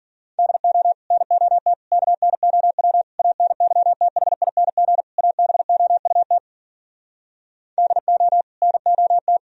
Morse code, heard as a keyed tone, about 58 wpm, 700 Hz. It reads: DO NOT KNOW ANYTHING ABOUT DO NOT